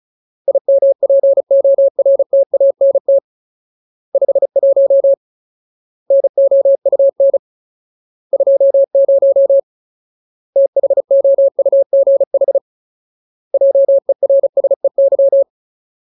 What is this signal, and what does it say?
Morse code, heard as a keyed tone, about 34 wpm, 550 Hz. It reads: IMPORTANT 51 NOUN 20 THOUGH JERSEY
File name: Ultimate Word List / 35WPM Ultimate